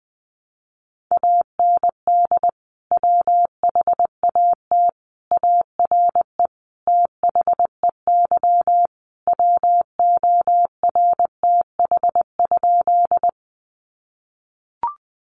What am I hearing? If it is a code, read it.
Morse code, heard as a keyed tone, about 20 wpm, 700 Hz. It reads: AND WHAT ARE THEY WORTH?
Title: Sentences from Top 800 Words 20wpm